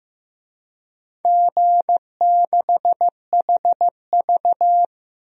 Morse code, heard as a keyed tone, about 15 wpm, 700 Hz. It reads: G6HV
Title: DXCC-100 Call Signs - 4 Characters - Repeated 3x - 15wpm